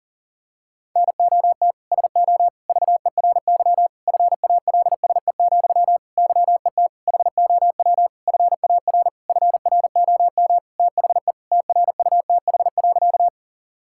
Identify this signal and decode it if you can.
Morse code, heard as a keyed tone, about 40 wpm, 700 Hz. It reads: NOT SO VERY FALSE, YET HOW FAR FROM THE TRUTH.
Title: Sentences from Top 1600 Words 40wpm